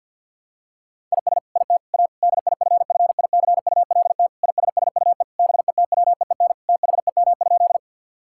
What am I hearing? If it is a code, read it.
Morse code, heard as a keyed tone, about 52 wpm, 700 Hz. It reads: IS IT A DIFFICULT ISSUE BETWEEN THEM?
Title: Sentences from Top 1000 Words - Review Part 6 - 50wpm